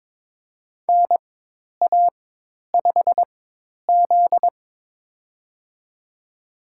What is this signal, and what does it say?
Morse code, heard as a keyed tone, about 22 wpm, spaced as for 10 wpm, 700 Hz. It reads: NA5Z
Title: USA Call Signs - 4 Characters - 4x Character Spacing - 22wpm